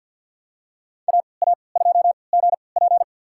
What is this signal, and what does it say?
Morse code, heard as a keyed tone, about 53 wpm, spaced as for 29 wpm, 700 Hz. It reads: AA2GP
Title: USA Call Signs - 5 Characters - 3x Character Spacing - 50wpm